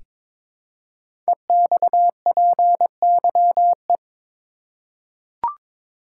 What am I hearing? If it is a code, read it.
Morse code, heard as a keyed tone, about 22 wpm, 700 Hz. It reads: EXPYE
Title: Code Groups - 5 Letters - 22wpm